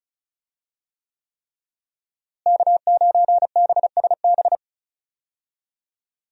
Morse code, heard as a keyed tone, about 34 wpm, 700 Hz. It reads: K9BSB
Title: USA Call Signs - 5 Characters - Repeated 3x - 35wpm